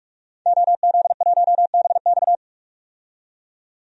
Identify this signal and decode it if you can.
Morse code, heard as a keyed tone, about 46 wpm, 700 Hz. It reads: OZ1BX
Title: DXCC-100 Call Signs - 5 Characters - 45wpm